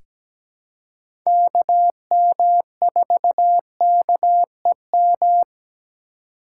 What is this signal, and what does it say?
Morse code, heard as a keyed tone, about 17 wpm, 700 Hz. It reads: KM4KEM
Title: USA Call Signs - 6 Characters - 3x Character Spacing - 17wpm